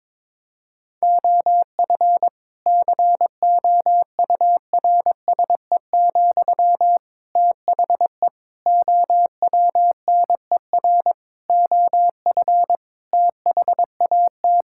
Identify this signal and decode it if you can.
Morse code, heard as a keyed tone, about 22 wpm, 700 Hz. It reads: OF COURSE, THE OWNER OF THAT
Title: Sentences from Top 1600 Words 22wpm